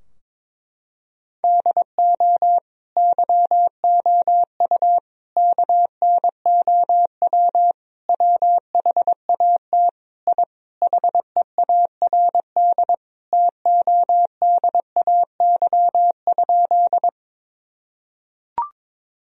Morse code, heard as a keyed tone, about 22 wpm, 700 Hz. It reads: DO YOU KNOW WHAT I HEARD TODAY?
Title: Sentences from Top 1000 Words - Review Part 4 - 22wpm